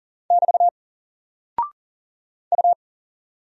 Morse code, heard as a keyed tone, about 40 wpm, 700 Hz. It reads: <BT> U
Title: Single Character (Letter:Number:Punctuation:Prosign) 40wpm